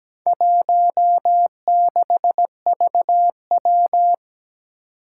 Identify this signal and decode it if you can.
Morse code, heard as a keyed tone, about 17 wpm, 700 Hz. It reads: 16VW
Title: Four Challenging Letter-Number Combinations 17wpm